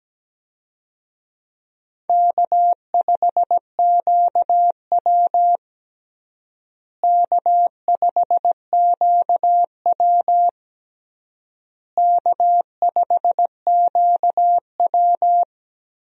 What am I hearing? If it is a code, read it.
Morse code, heard as a keyed tone, about 17 wpm, 700 Hz. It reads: K5QW K5QW K5QW
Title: USA Call Signs - 4 Characters - Repeated 3x - 17wpm